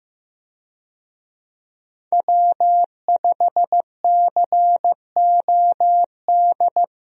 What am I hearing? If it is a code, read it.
Morse code, heard as a keyed tone, about 16 wpm, 700 Hz. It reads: W5COD
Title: USA Call Signs - 5 Characters - Repeated 3x - 15wpm